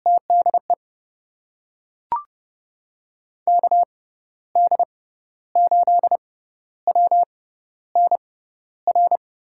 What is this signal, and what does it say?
Morse code, heard as a keyed tone, about 29 wpm, spaced as for 13 wpm, 700 Hz. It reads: TDE KD8WNR
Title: USA Call Signs - 6 Characters - 6x Character Spacing - 30wpm